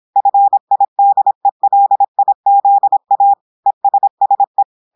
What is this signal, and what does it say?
Morse code, heard as a keyed tone, about 26 wpm, 800 Hz. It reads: FIDELIZA ESSE
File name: portugais texte en clair 2 vitesse 26 mots minutes